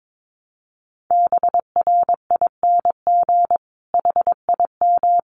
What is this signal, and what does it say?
Morse code, heard as a keyed tone, about 22 wpm, 700 Hz. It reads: BRING HIM
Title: Sentences from Top 1300 Words 22wpm